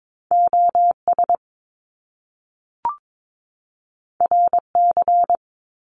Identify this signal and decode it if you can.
Morse code, heard as a keyed tone, about 22 wpm, 700 Hz. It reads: OS RC
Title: Top Two Letter Combinations in Words 22wpm